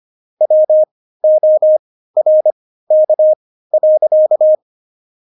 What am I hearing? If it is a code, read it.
Morse code, heard as a keyed tone, about 25 wpm, spaced as for 15 wpm, 620 Hz. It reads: WORK.
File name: Jokes-25-char-15-eff